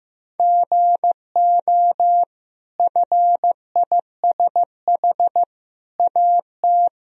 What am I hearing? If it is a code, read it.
Morse code, heard as a keyed tone, about 15 wpm, 700 Hz. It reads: GO FISH AT
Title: Sentences from Top 1000 Words - Review Part 1 - 15wpm